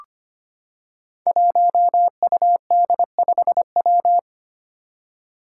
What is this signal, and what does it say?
Morse code, heard as a keyed tone, about 25 wpm, 700 Hz. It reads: 1UD5W
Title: Five Challenging Letter-Number Combinations 25wpm